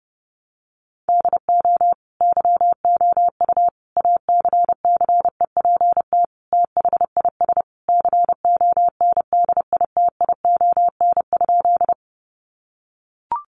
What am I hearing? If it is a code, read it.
Morse code, heard as a keyed tone, about 30 wpm, 700 Hz. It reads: DO YOU ACCEPT THIS CONDITION?
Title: Sentences from Top 1300 Words 30wpm